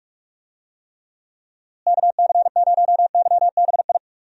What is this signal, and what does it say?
Morse code, heard as a keyed tone, about 46 wpm, 700 Hz. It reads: KC0YBI
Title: USA Call Signs - 6 Characters - Repeated 2x - 45wpm